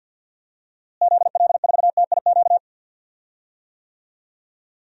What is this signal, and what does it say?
Morse code, heard as a keyed tone, about 47 wpm, 700 Hz. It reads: ZL4TIQ
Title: DXCC-100 Call Signs - 6 Characters - Repeated 3x - 50wpm